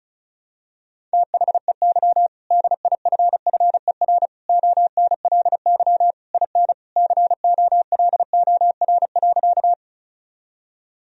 Morse code, heard as a keyed tone, about 34 wpm, 700 Hz. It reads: THEY DIFFER ONLY IN COLOR.
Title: Sentences from Top 1000 Words - Review Part 2 - 35wpm